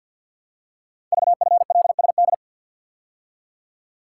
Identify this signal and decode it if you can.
Morse code, heard as a keyed tone, about 50 wpm, 690 Hz. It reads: VFLSD
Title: Code Groups - 5 Letters - 50wpm